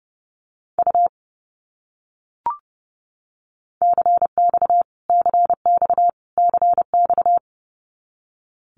Morse code, heard as a keyed tone, about 30 wpm, 700 Hz. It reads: U CX CX CX
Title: Abbreviations Repeated 3x - 30wpm